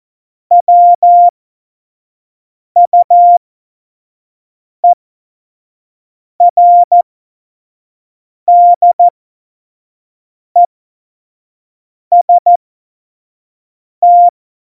Morse code, heard as a keyed tone, about 14 wpm, spaced as for 5 wpm, 700 Hz. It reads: WUERDEST
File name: Morsekurs / Witze / Witz 07. Januar 2020, 25 F70